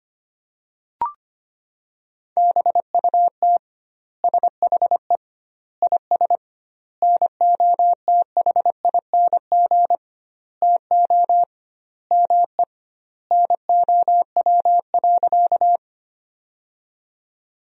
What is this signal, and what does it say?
Morse code, heard as a keyed tone, about 25 wpm, 700 Hz. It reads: BUT SHE IS NOTHING TO ME NOW.
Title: 2x Word-Spacing - Sentences from Top 500 Words - 25wpm